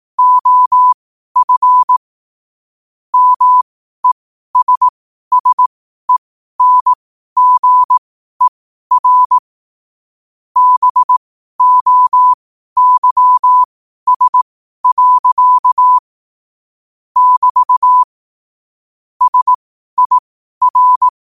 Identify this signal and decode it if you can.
Morse code, heard as a keyed tone, about 18 wpm, spaced as for 12 wpm, 1000 Hz. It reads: OF MESSENGER BOYS. <BT> SIR
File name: qotdfiles / 12wpm